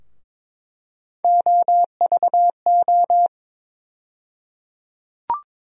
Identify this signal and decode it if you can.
Morse code, heard as a keyed tone, about 22 wpm, 700 Hz. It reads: OVO
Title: USA Call Sign Postfixes - 3 Letters 22wpm